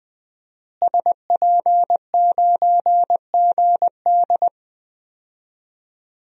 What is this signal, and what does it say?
Morse code, heard as a keyed tone, about 20 wpm, 700 Hz. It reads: SP9GD
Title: DXCC-100 Call Signs - 5 Characters - Repeated 3x - 20wpm